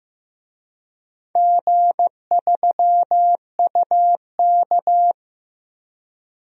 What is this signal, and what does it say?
Morse code, heard as a keyed tone, about 16 wpm, 700 Hz. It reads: G3UK
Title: DXCC-100 Call Signs - 4 Characters - Repeated 3x - 15wpm